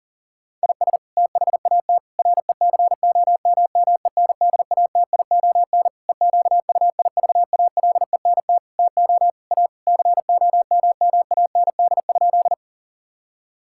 Morse code, heard as a keyed tone, about 40 wpm, 700 Hz. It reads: IS THAT RECOMMENDATION EQUIVALENT TO A COMMAND?
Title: Sentences from Top 1600 Words 40wpm